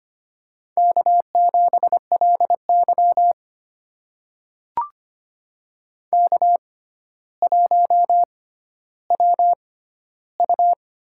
Morse code, heard as a keyed tone, about 25 wpm, spaced as for 13 wpm, 700 Hz. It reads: K7LY K1WU
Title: USA Call Signs - 6x Character Spacing - 25wpm